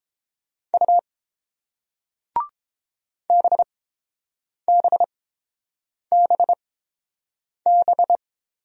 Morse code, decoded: U BBBB

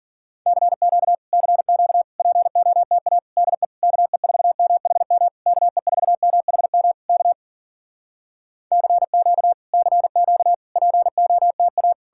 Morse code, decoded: CQ CQ POTA DE KE4MHM KE4MHM K CQ CQ POTA

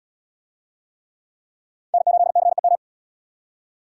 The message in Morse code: N8CEM

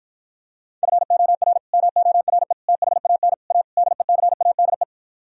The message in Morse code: FOR MORE THAN A DECADE

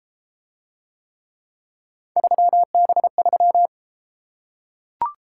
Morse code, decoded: 3B3

